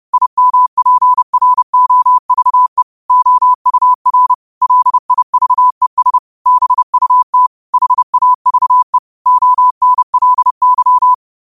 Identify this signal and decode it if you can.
Morse code, heard as a keyed tone, about 30 wpm, 1000 Hz. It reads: IMPROVE OUR LIVES BUT HAVE ONLY
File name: qotdfiles / 30wpm